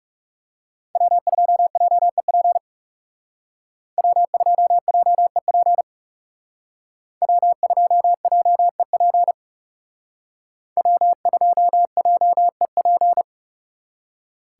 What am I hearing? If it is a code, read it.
Morse code, heard as a keyed tone, about 39 wpm, 700 Hz. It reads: W2JEP W2JEP W2JEP W2JEP